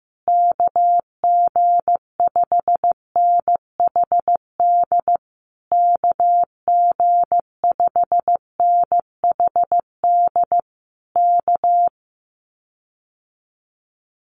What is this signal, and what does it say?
Morse code, heard as a keyed tone, about 15 wpm, 700 Hz. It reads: KG5NHD KG5NHD K
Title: Common QSO Phrases Encore 15wpm